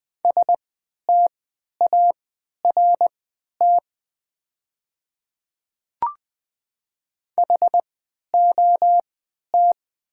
Morse code, heard as a keyed tone, about 20 wpm, spaced as for 11 wpm, 700 Hz. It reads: START HOT
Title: 3x Character Spacing - Top 500 Words - 20wpm